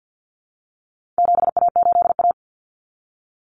Morse code, heard as a keyed tone, about 50 wpm, 700 Hz. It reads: 7A8A
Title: Common Cut Numbers - 4-Digit 50wpm